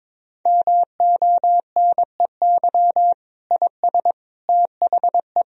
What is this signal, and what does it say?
Morse code, heard as a keyed tone, about 22 wpm, 700 Hz. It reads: MONEY IS THE
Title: Sentences from Top 1000 Words - Review Part 1 - 22wpm